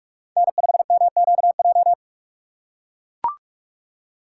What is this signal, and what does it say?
Morse code, heard as a keyed tone, about 44 wpm, 700 Hz. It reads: N5MQJ
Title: USA Call Signs - 2x Character Spacing - 45wpm